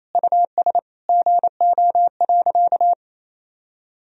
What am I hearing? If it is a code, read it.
Morse code, heard as a keyed tone, about 28 wpm, 700 Hz. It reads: US GO.
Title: Sentences from Top 1000 Words - Review Part 2 - 28wpm